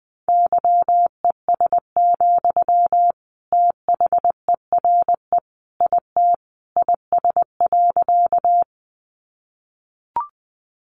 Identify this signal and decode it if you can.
Morse code, heard as a keyed tone, about 20 wpm, 700 Hz. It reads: YES, THERE IT IS.
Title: Sentences from Top 500 Words 20wpm